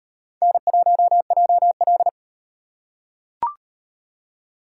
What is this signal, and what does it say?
Morse code, heard as a keyed tone, about 34 wpm, 700 Hz. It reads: N1JL